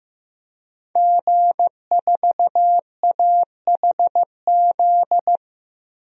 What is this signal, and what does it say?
Morse code, heard as a keyed tone, about 15 wpm, 700 Hz. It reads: G4AHZ
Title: DXCC-100 Call Signs - 5 Characters - 15wpm